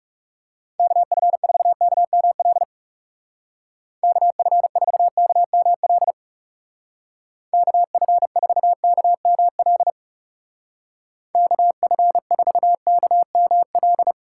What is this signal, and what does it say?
Morse code, heard as a keyed tone, about 40 wpm, 700 Hz. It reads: KF4KML KF4KML KF4KML KF4KML